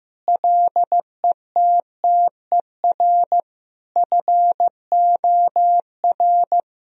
Morse code, heard as a keyed tone, about 15 wpm, 700 Hz. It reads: LETTER FOR
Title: Sentences from Top 1000 Words - Review Part 2 - 15wpm